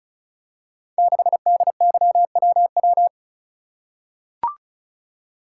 Morse code, heard as a keyed tone, about 36 wpm, 700 Hz. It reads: 6DYWW
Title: Code Groups - 5 Letter:Numbers - 35wpm